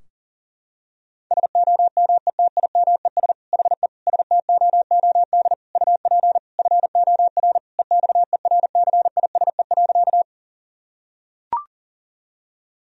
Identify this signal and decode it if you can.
Morse code, heard as a keyed tone, about 40 wpm, 700 Hz. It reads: SOMETIMES HE STOOD UP FOR EXERCISE.
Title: Sentences from Top 1000 Words - Review Part 4 - 40wpm